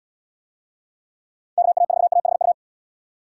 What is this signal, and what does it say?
Morse code, heard as a keyed tone, about 73 wpm, 700 Hz. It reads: KA4AHU